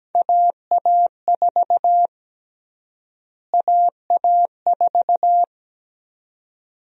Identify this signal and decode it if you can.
Morse code, heard as a keyed tone, about 17 wpm, 700 Hz. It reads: AA4 AA4
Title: USA Call Sign Prefixes - Repeated 2x - 17wpm